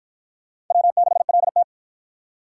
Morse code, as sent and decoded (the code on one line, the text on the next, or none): .-- -.... .-.. -
W6LT